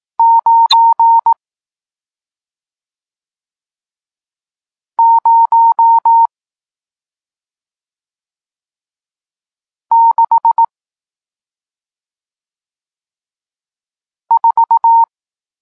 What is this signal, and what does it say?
Morse code, heard as a keyed tone, about 18 wpm, 920 Hz. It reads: 9 0 6 4
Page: Tekst 40 – Nauka znaków przestankowych